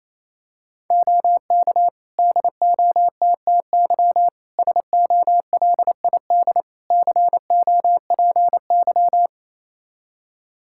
Morse code, heard as a keyed tone, about 28 wpm, 700 Hz. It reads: OK DOTTY SOLID COPY
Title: K1USN SST QSO Practice 28wpm